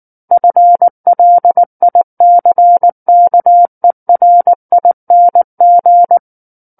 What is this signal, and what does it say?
Morse code, heard as a keyed tone, about 19 wpm, 700 Hz. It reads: FLICKERING
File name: Book One / 10-Chapter 10 - 19 WPM